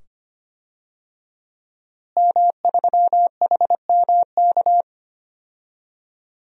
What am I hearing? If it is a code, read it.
Morse code, heard as a keyed tone, about 25 wpm, 700 Hz. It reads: M3HMK
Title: DXCC-100 Call Signs - 5 Characters - Repeated 2x - 25wpm